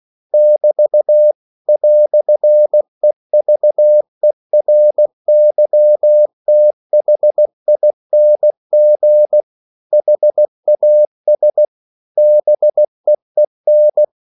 Morse code, heard as a keyed tone, about 16 wpm, 600 Hz. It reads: <BT> "EVERYTHING HAS BEEN